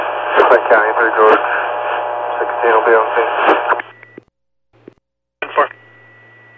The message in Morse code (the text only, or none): none